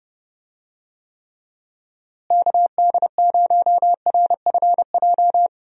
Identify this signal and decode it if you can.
Morse code, heard as a keyed tone, about 31 wpm, 700 Hz. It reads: KD0RFJ
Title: USA Call Signs - 6 Characters - Repeated 3x - 30wpm